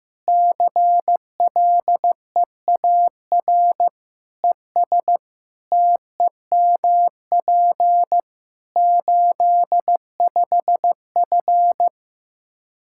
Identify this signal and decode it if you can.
Morse code, heard as a keyed tone, about 15 wpm, 700 Hz. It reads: CLEAR ES TEMP 85F
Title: Practice QSO - Exchange 2 - Info - 15wpm